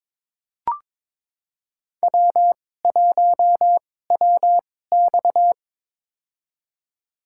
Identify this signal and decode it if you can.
Morse code, heard as a keyed tone, about 21 wpm, spaced as for 16 wpm, 700 Hz. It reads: W1WX
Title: USA Call Signs - 4 Characters - 2x Character Spacing - 22wpm